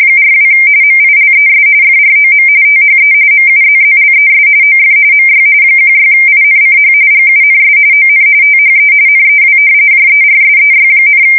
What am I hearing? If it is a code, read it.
RTTY, about 45.45 baud. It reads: 42 MILES FROM THE MUZZLES OF MY 16INCH GUNS ON THE NEW JERSEY,
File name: fsk170 / battleshipnj